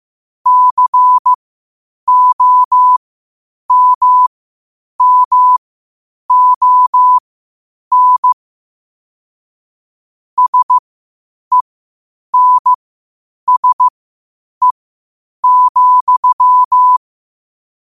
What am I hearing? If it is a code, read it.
Morse code, heard as a keyed tone, about 15 wpm, spaced as for 8 wpm, 1000 Hz. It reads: COMMON SENSE,
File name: qotdfiles / 8wpm